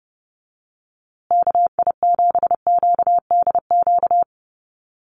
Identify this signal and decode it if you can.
Morse code, heard as a keyed tone, about 30 wpm, 700 Hz. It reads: KI7QDQ
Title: USA Call Signs Repeated 2x 30wpm